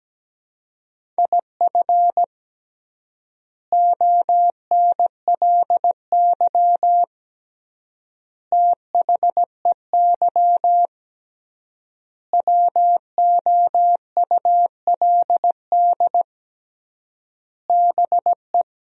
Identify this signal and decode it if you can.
Morse code, heard as a keyed tone, about 17 wpm, 700 Hz. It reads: IF ONLY THEY WOULD BE
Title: 3x Word-Spacing - Sentences from Top 500 Words - 17wpm